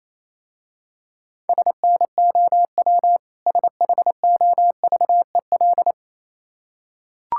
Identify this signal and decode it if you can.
Morse code, heard as a keyed tone, about 28 wpm, 700 Hz. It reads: SNOW SHOVEL
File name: Easy Words - Speed-Racing with Hints - 22wpm